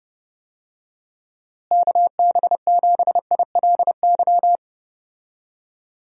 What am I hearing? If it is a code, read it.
Morse code, heard as a keyed tone, about 30 wpm, 700 Hz. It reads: KB7ILY
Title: USA Call Signs - 6 Characters - Repeated 3x - 30wpm